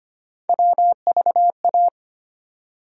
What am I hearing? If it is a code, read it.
Morse code, heard as a keyed tone, about 25 wpm, 700 Hz. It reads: WVA
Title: USA Call Sign Postfixes - 3 Letters 25wpm